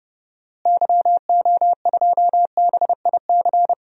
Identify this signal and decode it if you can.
Morse code, heard as a keyed tone, about 30 wpm, 700 Hz. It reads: YO2BIC
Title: DXCC-100 Call Signs - 6 Characters - Repeated 3x - 30wpm